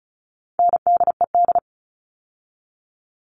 Morse code, heard as a keyed tone, about 34 wpm, 700 Hz. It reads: NDED
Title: All Cut Numbers - 4-Digit 35wpm